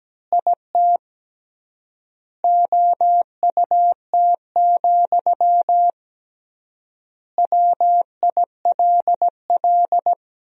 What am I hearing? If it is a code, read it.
Morse code, heard as a keyed tone, about 17 wpm, 700 Hz. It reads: IT OUT, WILL
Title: 3x Word-Spacing - Sentences from Top 300 Words - 17wpm